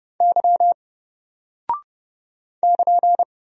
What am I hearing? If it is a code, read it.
Morse code, heard as a keyed tone, about 30 wpm, 700 Hz. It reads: Y <KN>
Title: Single Character (Letter:Number:Punctuation:Prosign) 30wpm